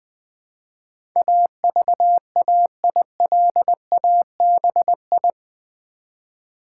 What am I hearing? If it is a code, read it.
Morse code, heard as a keyed tone, about 20 wpm, 700 Hz. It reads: AVAILABI
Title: Build A Word - 4 Letters At A Time - 12-Letter Words - 20wpm